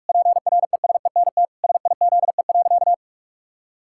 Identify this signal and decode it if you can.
Morse code, heard as a keyed tone, about 46 wpm, 700 Hz. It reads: PRESENT SIZE.
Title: Sentences from Top 1000 Words - Review Part 6 - 45wpm